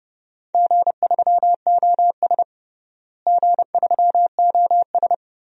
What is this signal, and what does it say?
Morse code, heard as a keyed tone, about 32 wpm, 700 Hz. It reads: G3OS G3OS
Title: DXCC-100 Call Signs - 4 Characters - Repeated 2x - 30wpm